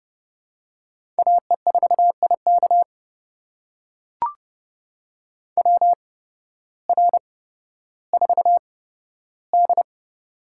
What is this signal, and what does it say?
Morse code, heard as a keyed tone, about 30 wpm, spaced as for 14 wpm, 700 Hz. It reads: AE4IK WR4D
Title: USA Call Signs - 8x Character Spacing - 30wpm